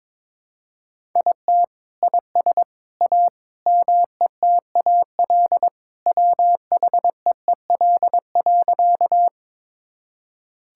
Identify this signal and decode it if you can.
Morse code, heard as a keyed tone, about 22 wpm, 700 Hz. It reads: IT IS A METAL WHEEL.
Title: Sentences from Top 1000 Words - Review Part 6 - 22wpm